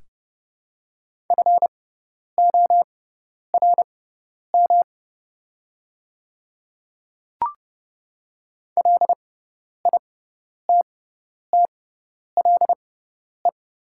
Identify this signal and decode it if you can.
Morse code, heard as a keyed tone, about 30 wpm, spaced as for 10 wpm, 690 Hz. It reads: FORM LITTLE
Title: 6x Character Spacing - Top 500 Words - 30wpm